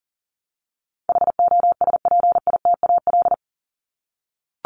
Morse code, heard as a keyed tone, about 40 wpm, 700 Hz. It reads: HOSPITAL
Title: Top 1751-2000 Words 40wpm